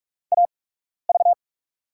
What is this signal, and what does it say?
Morse code, heard as a keyed tone, about 44 wpm, spaced as for 12 wpm, 690 Hz. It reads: AV
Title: USA Call Signs - 5 Characters - 8x Character Spacing - 45wpm